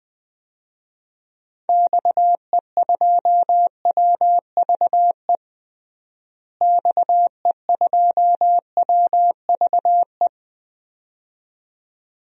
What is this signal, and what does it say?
Morse code, heard as a keyed tone, about 20 wpm, 700 Hz. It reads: XE2WVE XE2WVE
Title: DXCC-100 Call Signs - 6 Characters - Repeated 2x - 20wpm